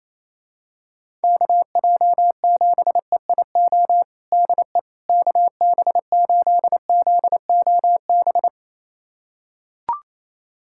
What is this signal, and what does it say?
Morse code, heard as a keyed tone, about 28 wpm, 700 Hz. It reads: KJ7EIO DE KB8ZOB
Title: USA Callsign DE Callsign 28wpm